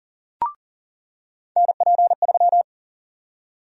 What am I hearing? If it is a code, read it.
Morse code, heard as a keyed tone, about 40 wpm, 700 Hz. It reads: NP3
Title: USA Call Sign Prefixes - 40wpm